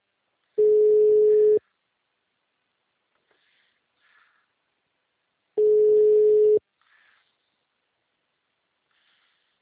Ringback tone (Europe, 425 Hz).